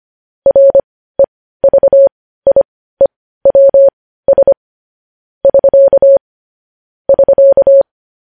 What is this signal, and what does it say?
Morse code, heard as a keyed tone, about 25 wpm, spaced as for 15 wpm, 550 Hz. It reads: REVIEWS <SK> <SK>
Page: news-headlines-in-morse-code-at-15-wpm